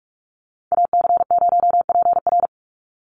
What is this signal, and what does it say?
Morse code, heard as a keyed tone, about 44 wpm, 700 Hz. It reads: AC0PR
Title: NA SOTA Chaser Callsigns 45wpm